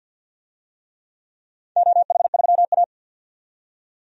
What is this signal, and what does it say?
Morse code, heard as a keyed tone, about 51 wpm, 700 Hz. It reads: OH3A
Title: DXCC-100 Call Signs - 4 Characters - 50wpm